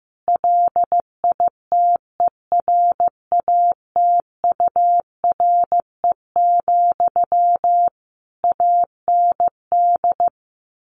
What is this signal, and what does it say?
Morse code, heard as a keyed tone, about 15 wpm, 700 Hz. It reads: LITERATURE, AND
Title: Sentences from Top 600 Words 15wpm